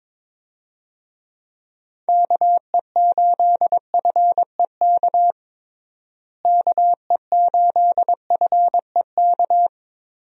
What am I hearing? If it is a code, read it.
Morse code, heard as a keyed tone, about 22 wpm, 700 Hz. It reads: KE8FEK KE8FEK
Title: USA Call Signs - 6 Characters - Repeated 3x - 22wpm